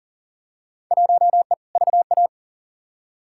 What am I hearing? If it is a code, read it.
Morse code, heard as a keyed tone, about 40 wpm, 700 Hz. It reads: 1E VA
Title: Field Day Exchanges - 40wpm